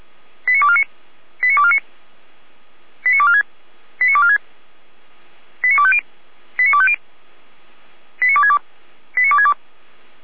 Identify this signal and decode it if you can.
Five-tone selective call (ZVEI), heard as quick runs of five tones, tones of about 70 ms. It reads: 89249 89249 89246 89246 89250 89250 89262 89262